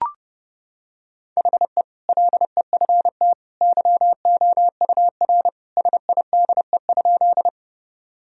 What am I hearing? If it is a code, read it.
Morse code, heard as a keyed tone, about 30 wpm, 700 Hz. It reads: HE LEFT YOUR SIDE?
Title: Sentences from Top 1000 Words - Review Part 1 - 30wpm